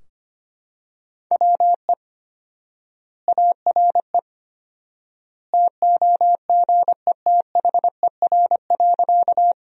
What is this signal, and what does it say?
Morse code, heard as a keyed tone, about 25 wpm, 700 Hz. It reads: WE ARE TOGETHER.
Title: 4x Word-Spacing - Sentences from Top 300 Words - 25wpm